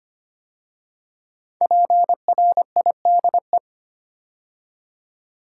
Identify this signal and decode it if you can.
Morse code, heard as a keyed tone, about 25 wpm, 700 Hz. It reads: PRIDE
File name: 5 Letter Words - First 3 Letters Given - 25wpm